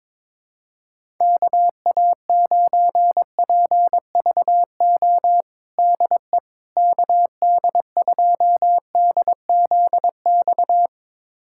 Morse code, heard as a keyed tone, about 22 wpm, 700 Hz. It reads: KA9PVO DE KD2DZX
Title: USA Callsign DE Callsign - 6 Characters - 22wpm